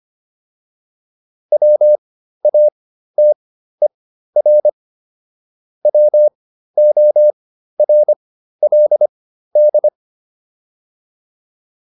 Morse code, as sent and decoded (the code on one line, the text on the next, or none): .-- .- - . .-. / .-- --- .-. .-.. -..
WATER WORLD